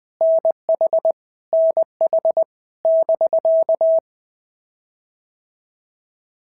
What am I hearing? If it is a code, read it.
Morse code, heard as a keyed tone, about 20 wpm, 650 Hz. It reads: NH NH <BK>
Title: POTA QSO Practice - 20wpm